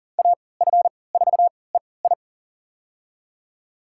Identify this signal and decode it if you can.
Morse code, heard as a keyed tone, about 40 wpm, spaced as for 23 wpm, 700 Hz. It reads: AF4EI
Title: USA Call Signs - 5 Characters - 3x Character Spacing - 40wpm